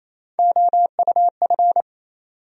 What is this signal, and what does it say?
Morse code, heard as a keyed tone, about 28 wpm, 700 Hz. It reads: OUF
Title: USA Call Sign Postfixes - 3 Letters 28wpm